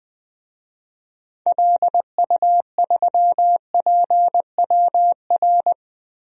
Morse code, heard as a keyed tone, about 20 wpm, 700 Hz. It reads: LU3PWR
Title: DXCC-100 Call Signs - 6 Characters - 20wpm